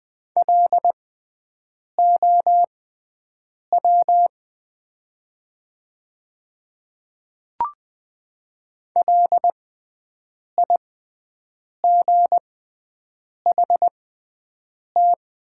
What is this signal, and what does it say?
Morse code, heard as a keyed tone, about 20 wpm, spaced as for 7 wpm, 700 Hz. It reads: LOW LIGHT
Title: 6x Character Spacing - Top 200 Words - 20wpm